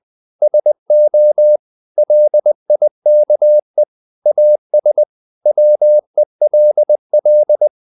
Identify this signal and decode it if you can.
Morse code, heard as a keyed tone, about 20 wpm, 600 Hz. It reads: SO LIKE AS WELL
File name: CW / Practice / 500MostCommonEnglishWords20